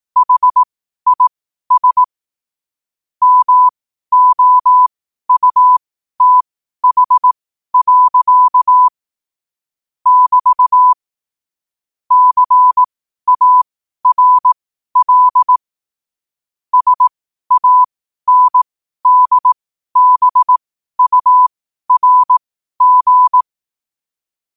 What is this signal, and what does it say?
Morse code, heard as a keyed tone, about 18 wpm, spaced as for 12 wpm, 1000 Hz. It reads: HIS MOUTH. <BT> CARL SANDBURG
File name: qotdfiles / 12wpm